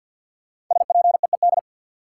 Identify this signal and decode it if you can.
Morse code, heard as a keyed tone, about 53 wpm, 700 Hz. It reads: SPEED
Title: Sentences from Top 800 Words - New Word Drill - 50wpm